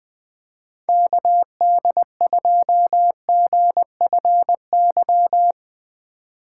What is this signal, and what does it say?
Morse code, heard as a keyed tone, about 20 wpm, 700 Hz. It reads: KD2GFY
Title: USA Call Signs - 2x Character Spacing - 20wpm